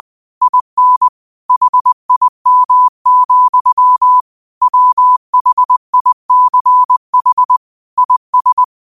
Morse code, heard as a keyed tone, about 20 wpm, 1000 Hz. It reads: IN HIM, WHICH IS